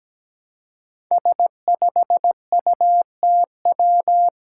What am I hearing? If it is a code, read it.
Morse code, heard as a keyed tone, about 17 wpm, 700 Hz. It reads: S5UTW
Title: Five Challenging Letter-Number Combinations 17wpm